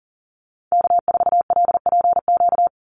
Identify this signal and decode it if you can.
Morse code, heard as a keyed tone, about 38 wpm, 700 Hz. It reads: K4LPQ
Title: NA SOTA Activator Callsigns 40wpm